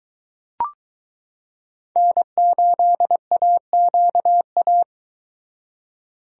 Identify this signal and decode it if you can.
Morse code, heard as a keyed tone, about 22 wpm, 700 Hz. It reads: N8AQA